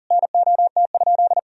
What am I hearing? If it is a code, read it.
Morse code, heard as a keyed tone, about 37 wpm, 700 Hz. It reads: NOT?
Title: Sentences from Top 1000 Words - Review Part 3 - 40wpm